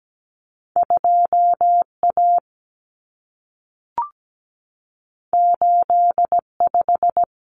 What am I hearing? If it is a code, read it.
Morse code, heard as a keyed tone, about 17 wpm, 700 Hz. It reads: 2A 85
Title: Common Cut Numbers - 2-Digit 17wpm